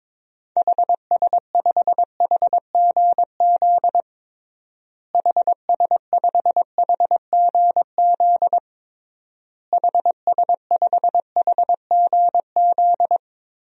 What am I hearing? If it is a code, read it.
Morse code, heard as a keyed tone, about 22 wpm, 700 Hz. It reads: HS5HGZ HS5HGZ HS5HGZ
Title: DXCC-100 Call Signs - 6 Characters - Repeated 3x - 22wpm